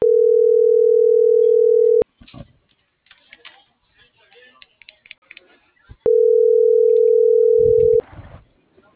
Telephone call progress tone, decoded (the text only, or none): ringback tone